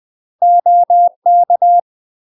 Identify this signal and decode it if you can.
Morse code, heard as a keyed tone, about 20 wpm, 700 Hz. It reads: OK
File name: UK MASTER QSOs TRAINING / 20WPM / UK MASTER QSOs 0065